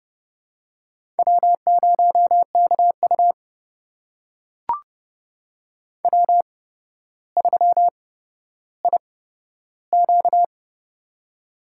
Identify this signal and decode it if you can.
Morse code, heard as a keyed tone, about 31 wpm, spaced as for 13 wpm, 700 Hz. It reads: W0KU W3IQ
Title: USA Call Signs - 4 Characters - 8x Character Spacing - 30wpm